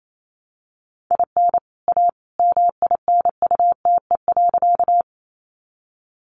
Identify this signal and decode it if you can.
Morse code, heard as a keyed tone, about 28 wpm, 700 Hz. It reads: IN A MINUTE.
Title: Sentences from Top 500 Words 28wpm